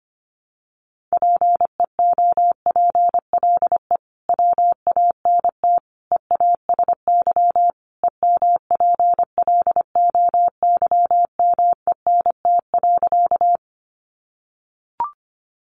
Morse code, heard as a keyed tone, about 25 wpm, 700 Hz. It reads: PEOPLE WANT EASY EMPLOYMENT.
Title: Sentences from Top 1500 Words 25wpm